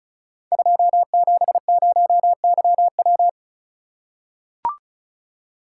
Morse code, heard as a keyed tone, about 34 wpm, 700 Hz. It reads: 270YW
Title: Code Groups - 5 Letter:Numbers - 35wpm